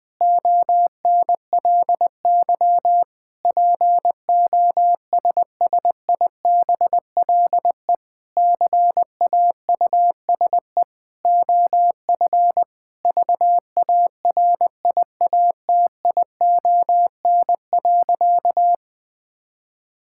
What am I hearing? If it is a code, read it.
Morse code, heard as a keyed tone, about 20 wpm, 700 Hz. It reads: ONLY POSSIBLE CAUSE OF VARIATION.
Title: Sentences from Top 1600 Words 20wpm